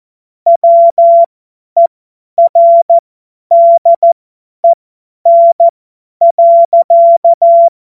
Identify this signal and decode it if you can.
Morse code, heard as a keyed tone, about 14 wpm, spaced as for 10 wpm, 680 Hz. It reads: WERDEN.
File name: Morsekurs / Witze / Witz 27. Juli 2021, 50 F70